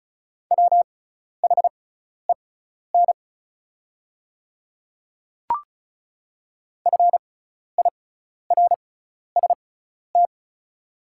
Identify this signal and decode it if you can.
Morse code, heard as a keyed tone, about 34 wpm, spaced as for 12 wpm, 700 Hz. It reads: WHEN FIRST
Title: 6x Character Spacing - Top 300 Words - 35wpm